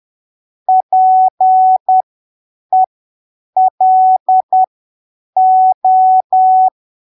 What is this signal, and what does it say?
Morse code, heard as a keyed tone, about 10 wpm, spaced as for 7 wpm, 750 Hz. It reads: PELO
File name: arquivos / texto / texto 14 10ppm